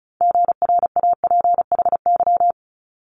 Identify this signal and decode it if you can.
Morse code, heard as a keyed tone, about 36 wpm, 700 Hz. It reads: GRAPHY
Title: Multi-letter Phonemes Quick Fire 35wpm